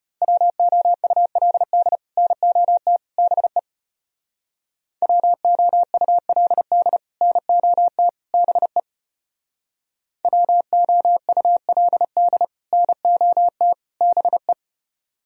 Morse code, heard as a keyed tone, about 33 wpm, 700 Hz. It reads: WOULD NOT BE WOULD NOT BE WOULD NOT BE